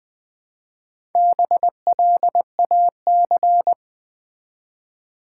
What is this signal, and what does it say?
Morse code, heard as a keyed tone, about 20 wpm, 700 Hz. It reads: BLAC
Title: Build A Word - 1 Letter At A Time - 5-Letter Words - 20wpm